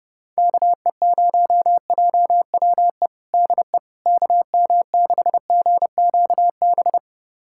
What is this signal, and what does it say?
Morse code, heard as a keyed tone, about 30 wpm, 700 Hz. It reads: KE0JWE DE KM6GQB
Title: USA Callsign DE Callsign 30wpm